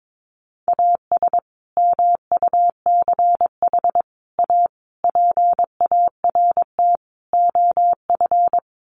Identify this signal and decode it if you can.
Morse code, heard as a keyed tone, about 22 wpm, 700 Hz. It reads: AS MUCH A PART OF
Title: Sets of 5 Words Encore2 22wpm